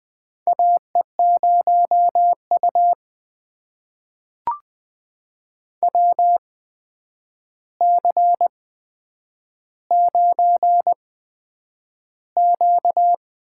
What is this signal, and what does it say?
Morse code, heard as a keyed tone, about 20 wpm, spaced as for 9 wpm, 700 Hz. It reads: AE0U WC9Q
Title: USA Call Signs - 4 Characters - 8x Character Spacing - 20wpm